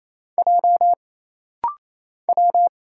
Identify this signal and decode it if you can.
Morse code, heard as a keyed tone, about 28 wpm, 700 Hz. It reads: J W
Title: Single Letter-Number Alt - Rapid-Fire 28wpm